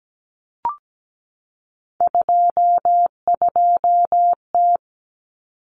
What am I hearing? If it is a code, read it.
Morse code, heard as a keyed tone, about 17 wpm, 690 Hz. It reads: 22T
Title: Common Cut Numbers - 3-Digit 17wpm